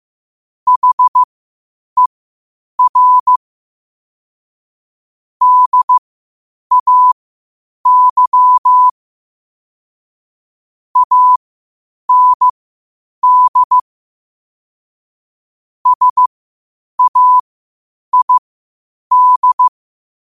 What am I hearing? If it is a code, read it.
Morse code, heard as a keyed tone, about 15 wpm, spaced as for 8 wpm, 1000 Hz. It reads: HER DAY AND SAID
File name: qotdfiles / 8wpm